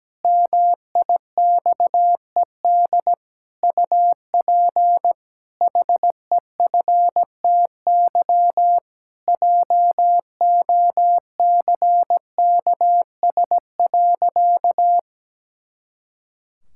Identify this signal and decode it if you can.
Morse code, heard as a keyed tone, about 17 wpm, 700 Hz. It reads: MIXED UP HEFTY JOCKS.
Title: Pangram Challenge v2 - 17wpm